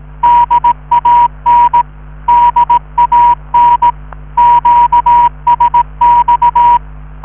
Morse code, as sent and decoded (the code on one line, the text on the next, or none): -.. .- -. / -.. .- -. / --.- ... -..-
DAN DAN QSX